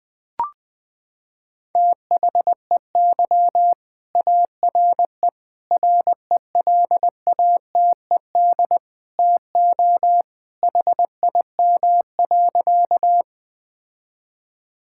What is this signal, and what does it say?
Morse code, heard as a keyed tone, about 21 wpm, 700 Hz. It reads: THEY ARE RELATED TO HIM.
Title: Sentences from Top 1000 Words - Review Part 5 - 20wpm